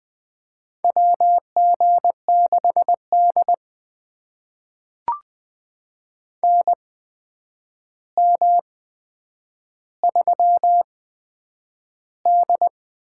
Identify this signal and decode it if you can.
Morse code, heard as a keyed tone, about 20 wpm, spaced as for 9 wpm, 700 Hz. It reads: WG6D NM3D
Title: USA Call Signs - 4 Characters - 8x Character Spacing - 20wpm